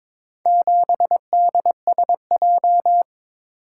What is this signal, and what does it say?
Morse code, heard as a keyed tone, about 22 wpm, 700 Hz. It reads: 7DSJ
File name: practice / 22wpm / Four Letter-Number Combinations - 22wpm